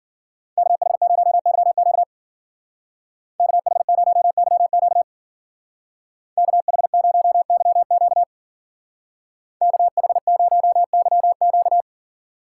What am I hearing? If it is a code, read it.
Morse code, heard as a keyed tone, about 50 wpm, 700 Hz. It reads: KH0YQ KH0YQ KH0YQ KH0YQ
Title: Speed Racing - USA 2x2 Call Signs 40wpm